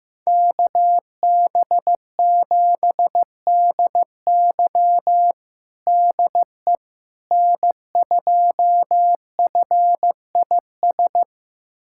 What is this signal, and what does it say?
Morse code, heard as a keyed tone, about 15 wpm, 700 Hz. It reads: KB7DY DE N2FIS
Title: USA Callsign DE Callsign - 5 Characters - 15wpm